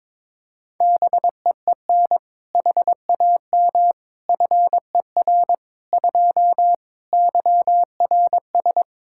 Morse code, heard as a keyed tone, about 22 wpm, 700 Hz. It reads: BEEN HAM FER 2 YRS
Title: Practice QSO - Exchange 3 - More Info - 22wpm